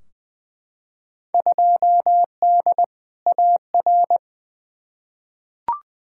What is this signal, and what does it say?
Morse code, heard as a keyed tone, about 20 wpm, 700 Hz. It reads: 2D AR
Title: Field Day Exchanges Encore - 20wpm